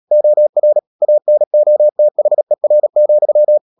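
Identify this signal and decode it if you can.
Morse code, heard as a keyed tone, about 37 wpm, 600 Hz. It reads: OR ANOTHER,
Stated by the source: Morse code practice at 35–39 wpm sending English text of Ken Follett's novel Never, from CW med Gnister.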